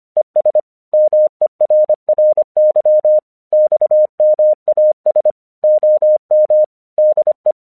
Morse code, heard as a keyed tone, about 26 wpm, 610 Hz. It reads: ES MERRY XMAS OM DE
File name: 1 English 25wpm / AQC29